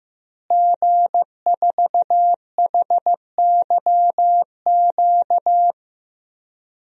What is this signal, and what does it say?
Morse code, heard as a keyed tone, about 15 wpm, 700 Hz. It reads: G4HYQ
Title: DXCC-100 Call Signs - 5 Characters - Repeated 2x - 15wpm